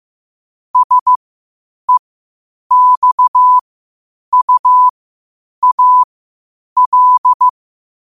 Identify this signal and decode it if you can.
Morse code, heard as a keyed tone, about 15 wpm, spaced as for 8 wpm, 1000 Hz. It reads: SEXUAL
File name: qotdfiles / 8wpm